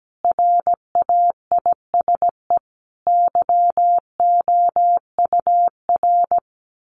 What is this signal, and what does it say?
Morse code, heard as a keyed tone, about 17 wpm, 700 Hz. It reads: RAISE YOUR
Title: Sentences from Top 1400 Words 17wpm